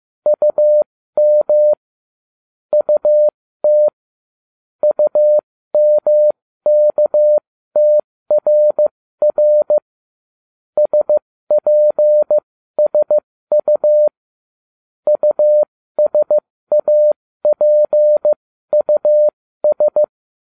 Morse code, heard as a keyed tone, about 15 wpm, spaced as for 13 wpm, 600 Hz. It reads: UM UT UMKTRR SPSU USAPUS